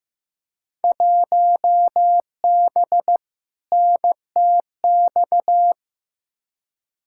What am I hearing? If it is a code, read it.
Morse code, heard as a keyed tone, about 15 wpm, 700 Hz. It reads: 1B NTX
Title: Field Day Exchanges - 15wpm